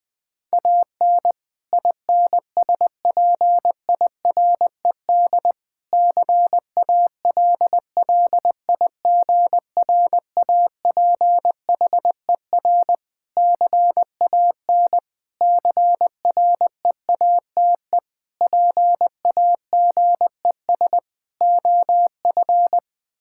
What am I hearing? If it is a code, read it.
Morse code, heard as a keyed tone, about 20 wpm, 700 Hz. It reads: AN INSPIRED CALLIGRAPHER CAN CREATE PAGES OF
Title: Pangram Challenge v2 - 20wpm